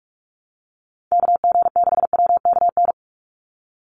Morse code, heard as a keyed tone, about 44 wpm, 700 Hz. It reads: KG6WKN
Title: USA Call Signs 45wpm